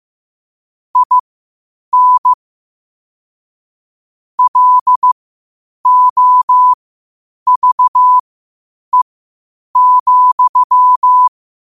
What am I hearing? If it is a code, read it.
Morse code, heard as a keyed tone, about 15 wpm, spaced as for 8 wpm, 1000 Hz. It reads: IN LOVE,
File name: qotdfiles / 8wpm